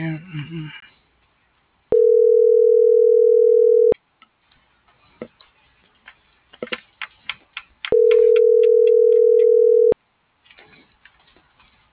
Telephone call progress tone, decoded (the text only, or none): ringback tone